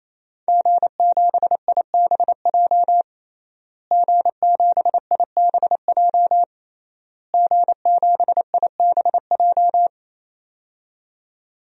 Morse code, heard as a keyed tone, about 27 wpm, 700 Hz. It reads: G7IBJ G7IBJ G7IBJ
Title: DXCC-100 Call Signs - 5 Characters - Repeated 3x - 28wpm